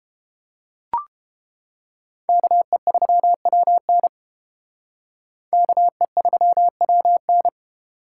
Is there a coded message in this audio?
Yes: Morse code from 2.3 to 7.5 s, heard as a keyed tone, about 33 wpm, 700 Hz. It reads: KE3WN KE3WN